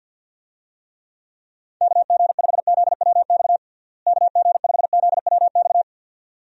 Morse code, heard as a keyed tone, about 52 wpm, 700 Hz. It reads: KG5ZWX KG5ZWX
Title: USA Call Signs - 6 Characters - Repeated 2x - 50wpm